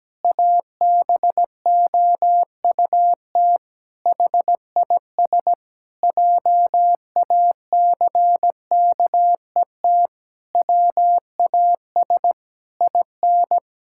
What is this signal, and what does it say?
Morse code, heard as a keyed tone, about 17 wpm, 700 Hz. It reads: ABOUT HIS JACKET WAS IN
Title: Pangram Challenge v2 - 17wpm